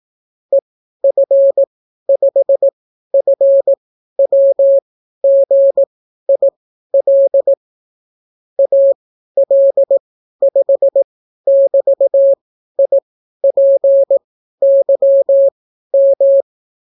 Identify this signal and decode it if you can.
Morse code, heard as a keyed tone, about 18 wpm, spaced as for 12 wpm, 550 Hz. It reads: EF5FWGIL AL5<BT>IPYM